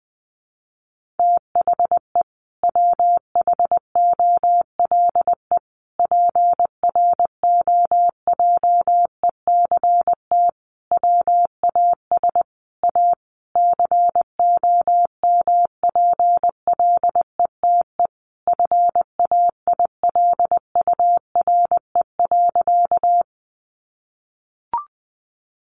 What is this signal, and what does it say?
Morse code, heard as a keyed tone, about 19 wpm, 700 Hz. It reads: THE WHOLE PROJECT WAS A COMPLETE FAILURE.
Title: Sentences from Top 1400 Words 20wpm